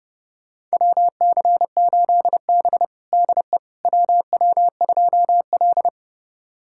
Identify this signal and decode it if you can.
Morse code, heard as a keyed tone, about 32 wpm, 700 Hz. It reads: WC8B DE WW2L
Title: USA Callsign DE Callsign - 4 Characters - 30wpm